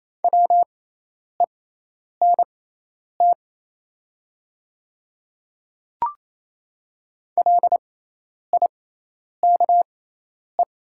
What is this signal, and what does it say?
Morse code, heard as a keyed tone, about 28 wpm, spaced as for 10 wpm, 700 Hz. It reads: WENT LIKE
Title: 6x Character Spacing - Top 200 Words - 28wpm